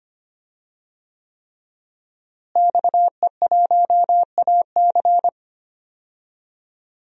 Morse code, heard as a keyed tone, about 25 wpm, 700 Hz. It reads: XE1AC
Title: DXCC-100 Call Signs - 5 Characters - Repeated 3x - 25wpm